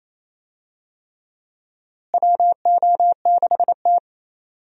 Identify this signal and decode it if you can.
Morse code, heard as a keyed tone, about 28 wpm, 700 Hz. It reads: WO6T
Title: USA Call Signs - 4 Characters - Repeated 3x - 28wpm